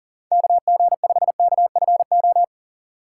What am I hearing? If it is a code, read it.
Morse code, heard as a keyed tone, about 40 wpm, 700 Hz. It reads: KG5KFO
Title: USA Call Signs - 6 Characters - 4x Character Spacing - 40wpm